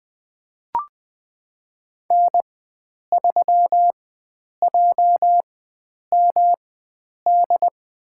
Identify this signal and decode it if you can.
Morse code, heard as a keyed tone, about 20 wpm, spaced as for 9 wpm, 700 Hz. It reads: N3JMD
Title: USA Call Signs - 4x Character Spacing - 20wpm